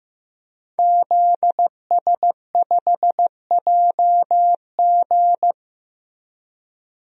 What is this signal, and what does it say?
Morse code, heard as a keyed tone, about 15 wpm, 700 Hz. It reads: ZS5JG
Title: DXCC-100 Call Signs - 5 Characters - Repeated 3x - 15wpm